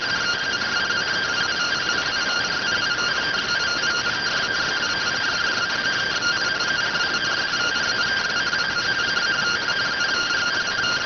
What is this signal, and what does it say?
Two-tone teleprinter signal, about 170 Hz shift.